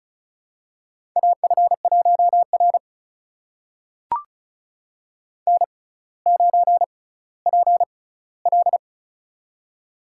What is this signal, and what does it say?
Morse code, heard as a keyed tone, about 36 wpm, spaced as for 18 wpm, 700 Hz. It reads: AF1R N9PL
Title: USA Call Signs - 4 Characters - 6x Character Spacing - 35wpm